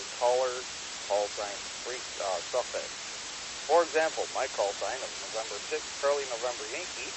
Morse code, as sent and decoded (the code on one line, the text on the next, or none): none